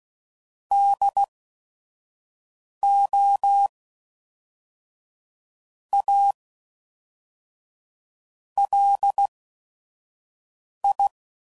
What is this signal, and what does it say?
Morse code, heard as a keyed tone, about 16 wpm, spaced as for 5 wpm, 800 Hz. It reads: DO A LI
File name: CW Code Groups @5WPM / QSOs / QSO WD0FBT